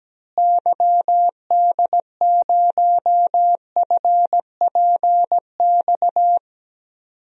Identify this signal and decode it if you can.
Morse code, heard as a keyed tone, about 17 wpm, 700 Hz. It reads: YD0FPX
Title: DXCC-100 Call Signs - 6 Characters - Repeated 3x - 17wpm